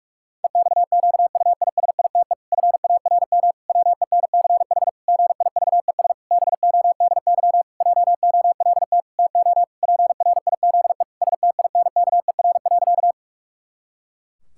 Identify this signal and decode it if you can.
Morse code, heard as a keyed tone, about 46 wpm, 700 Hz. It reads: EXQUISITE FARM WENCH GIVES BODY JOLT TO PRIZE STINKER.
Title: Pangram Challenge v2 - 45wpm